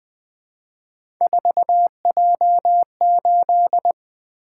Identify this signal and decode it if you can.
Morse code, heard as a keyed tone, about 20 wpm, 700 Hz. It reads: 4J8
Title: Three Letter-Number Combinations - 20wpm